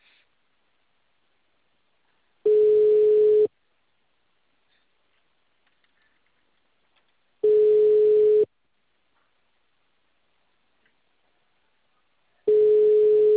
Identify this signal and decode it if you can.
Ringback tone (Europe, 425 Hz).
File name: out- -2055- - - .43690